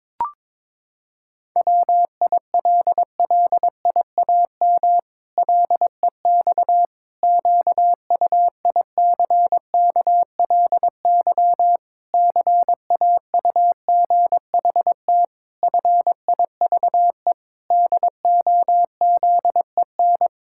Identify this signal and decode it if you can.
Morse code, heard as a keyed tone, about 22 wpm, 700 Hz. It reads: WILLIAM LEX QUICKLY CAUGHT FIVE DOZEN
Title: Pangram Challenge v2 - 22wpm